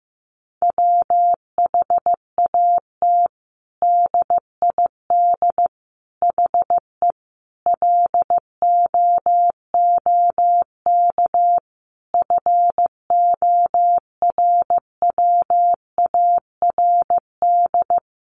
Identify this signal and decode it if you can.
Morse code, heard as a keyed tone, about 15 wpm, 700 Hz. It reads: WHAT DID HE LOOK FORWARD
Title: Sentences from Top 1300 Words 15wpm